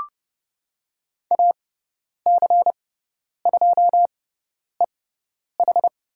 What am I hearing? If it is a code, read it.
Morse code, heard as a keyed tone, about 30 wpm, spaced as for 10 wpm, 700 Hz. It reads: AC2EH
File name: Lesson 16g - Letter c - Call Signs 30wpm